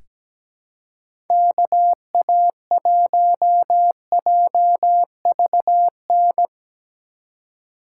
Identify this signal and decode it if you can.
Morse code, heard as a keyed tone, about 17 wpm, 700 Hz. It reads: KA1JVN
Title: USA Call Signs - 6 Characters - 4x Character Spacing - 17wpm